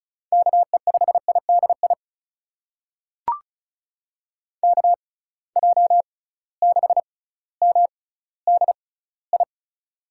Morse code, heard as a keyed tone, about 36 wpm, spaced as for 18 wpm, 700 Hz. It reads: KE5IDI KJ6MDI
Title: USA Call Signs - 6 Characters - 6x Character Spacing - 35wpm